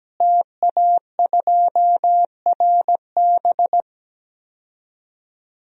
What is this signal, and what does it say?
Morse code, heard as a keyed tone, about 17 wpm, 700 Hz. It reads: TA2RB
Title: DXCC-100 Call Signs - 5 Characters - Repeated 2x - 17wpm